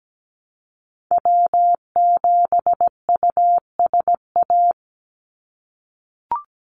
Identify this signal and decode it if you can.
Morse code, heard as a keyed tone, about 17 wpm, 700 Hz. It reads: W7USA
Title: NA SOTA Activator Callsigns 17wpm